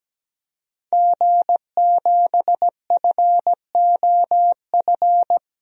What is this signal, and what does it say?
Morse code, heard as a keyed tone, about 17 wpm, 700 Hz. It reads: G7FOF
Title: DXCC-100 Call Signs - 5 Characters - 17wpm